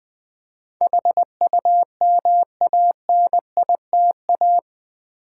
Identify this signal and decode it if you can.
Morse code, heard as a keyed tone, about 20 wpm, 700 Hz. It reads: HUMANITA
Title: Build A Word - 4 Letters At A Time - 12-Letter Words - 20wpm